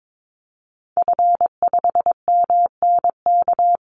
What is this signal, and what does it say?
Morse code, heard as a keyed tone, about 22 wpm, 700 Hz. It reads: F5MNK
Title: Call Signs - 22wpm